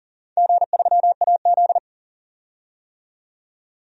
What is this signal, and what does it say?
Morse code, heard as a keyed tone, about 40 wpm, 700 Hz. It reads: G3AZ
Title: DXCC-100 Call Signs - 4 Characters - 40wpm